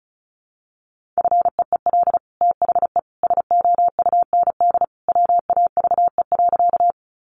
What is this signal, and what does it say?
Morse code, heard as a keyed tone, about 34 wpm, 700 Hz. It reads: FEEL THE SOUND WAVE.
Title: Sentences from Top 1400 Words 35wpm